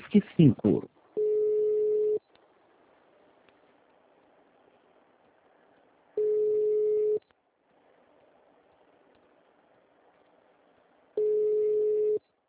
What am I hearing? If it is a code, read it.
Ringback tone (Europe, 425 Hz), with speech.
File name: out- -2009- - - .4416